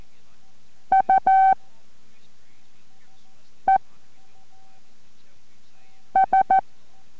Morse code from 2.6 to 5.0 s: E